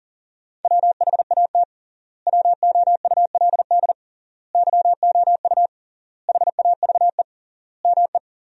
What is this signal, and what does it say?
Morse code, heard as a keyed tone, about 40 wpm, 700 Hz. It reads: WHAT WOULD YOU HAVE ME
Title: 3x Word-Spacing - Sentences from Top 200 Words - 40wpm